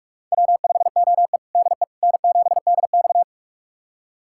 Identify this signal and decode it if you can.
Morse code, heard as a keyed tone, about 46 wpm, 700 Hz. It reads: W5OE DE N7DX
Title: USA Callsign DE Callsign - 4 Characters - 45wpm